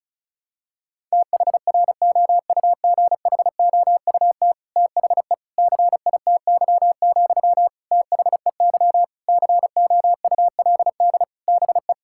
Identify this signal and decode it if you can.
Morse code, heard as a keyed tone, about 34 wpm, 700 Hz. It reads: THROUGHOUT THE CITY, THEY COULD BE
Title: Sentences from Top 1000 Words - Review Part 5 - 35wpm